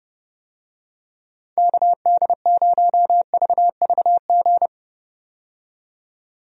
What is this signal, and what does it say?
Morse code, heard as a keyed tone, about 30 wpm, 700 Hz. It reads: KD0VVG
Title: USA Call Signs - 6 Characters - 30wpm